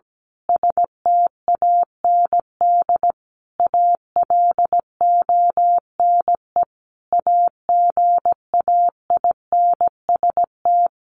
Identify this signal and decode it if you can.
Morse code, heard as a keyed tone, about 17 wpm, 700 Hz. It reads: STAND ALONE AGAINST
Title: Sentences from Top 1400 Words 17wpm